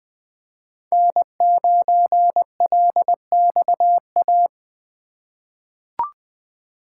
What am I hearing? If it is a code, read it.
Morse code, heard as a keyed tone, about 20 wpm, 700 Hz. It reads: N9LXA
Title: USA Call Signs - 5 Characters - 3x Character Spacing - 20wpm